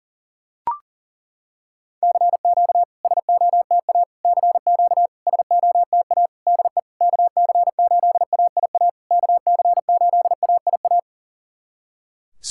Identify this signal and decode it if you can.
Morse code, heard as a keyed tone, about 40 wpm, 690 Hz. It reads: CQ SOTA CQ SOTA DE KC8AIA KC8AIA
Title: SOTA QSO Practice 40wpm